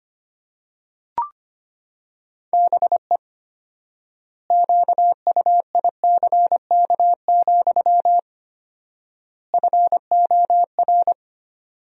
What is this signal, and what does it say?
Morse code, heard as a keyed tone, about 25 wpm, 700 Hz. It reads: BE QUICK, FOR
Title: 4x Word-Spacing - Sentences from Top 500 Words - 25wpm